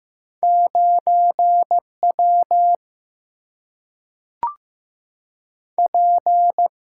Morse code, heard as a keyed tone, about 15 wpm, 700 Hz. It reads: 9 W P
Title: Single-Letter Single-Number 15wpm